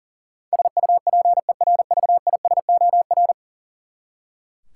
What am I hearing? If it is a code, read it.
Morse code, heard as a keyed tone, about 40 wpm, 700 Hz. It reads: SUPERVISOR
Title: Practice QSO - Exchange 3 - More Info - 40wpm